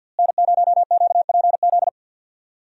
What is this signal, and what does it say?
Morse code, heard as a keyed tone, about 46 wpm, 700 Hz. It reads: N0QPZ
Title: USA Call Signs - 5 Characters - Repeated 3x - 50wpm